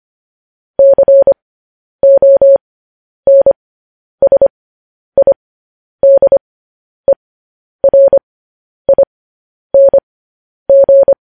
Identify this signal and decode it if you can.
Morse code, heard as a keyed tone, about 25 wpm, spaced as for 10 wpm, 550 Hz. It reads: CONSIDERING
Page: News Headlines in Morse Code at 10 WPM